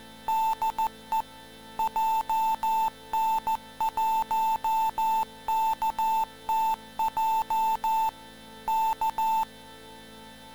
Morse code, heard as a keyed tone, about 14 wpm, 890 Hz. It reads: DE JN1KTJ K
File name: TaitoChaseHQ-MorseCode